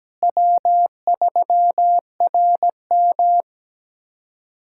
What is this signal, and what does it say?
Morse code, heard as a keyed tone, about 17 wpm, 700 Hz. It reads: W3RM
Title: USA Callsign DE Callsign - 4 Characters - 17wpm